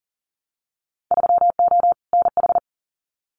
Morse code, heard as a keyed tone, about 40 wpm, 700 Hz. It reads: 3O NH
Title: Winter Field Day Exchanges Encore 40wpm